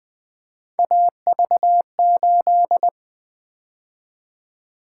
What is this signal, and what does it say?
Morse code, heard as a keyed tone, about 24 wpm, 700 Hz. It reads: AV8
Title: Speed-Racing - Three Letter-Number Combinations - 20wpm